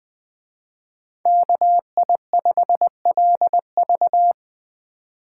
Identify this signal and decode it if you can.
Morse code, heard as a keyed tone, about 20 wpm, 700 Hz. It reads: KI5LV
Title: Speed Racing - USA 2x2 Call Signs 17wpm